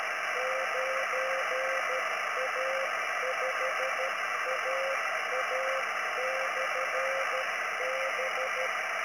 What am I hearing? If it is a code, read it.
Morse code, heard as a keyed tone, about 13 wpm, 520 Hz, with radio noise. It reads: 9A5AA/B